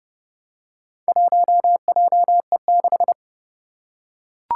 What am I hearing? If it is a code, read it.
Morse code, heard as a keyed tone, about 30 wpm, 700 Hz. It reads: 1JE6
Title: Four Challenging Letter-Number Combinations 30wpm